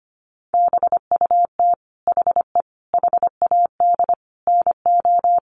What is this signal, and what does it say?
Morse code, heard as a keyed tone, about 25 wpm, 700 Hz. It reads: BUT HE HAD NO
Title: Sentences from Top 100 Words 25wpm